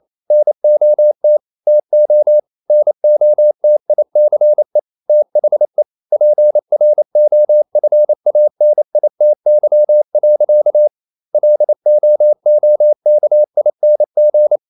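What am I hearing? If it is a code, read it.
Morse code, heard as a keyed tone, about 28 wpm, 600 Hz. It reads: NOT TO NOTICE THE PROFANITY. LOOKING
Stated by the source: Morse code practice at 28 wpm, text of Ken Follett's novel Never in English, from CW med Gnister.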